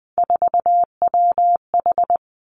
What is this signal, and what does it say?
Morse code, heard as a keyed tone, about 19 wpm, 700 Hz. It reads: 4WH
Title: Three Challenging Letter-Number Combinations 20wpm